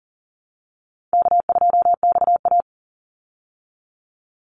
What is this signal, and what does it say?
Morse code, heard as a keyed tone, about 40 wpm, 700 Hz. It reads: K2XA
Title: Easy Call Signs Repeated 3x - 40wpm